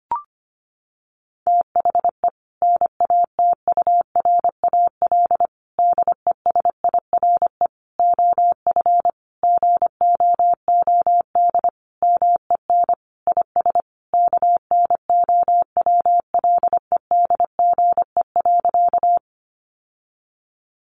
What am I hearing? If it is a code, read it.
Morse code, heard as a keyed tone, about 25 wpm, 700 Hz. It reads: THE NATURAL DESIRE OF GOOD MEN IS KNOWLEDGE.
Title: Sentences from Top 1100 Words 25wpm